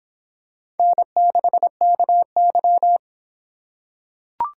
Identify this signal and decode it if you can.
Morse code, heard as a keyed tone, about 26 wpm, 700 Hz. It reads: N6KY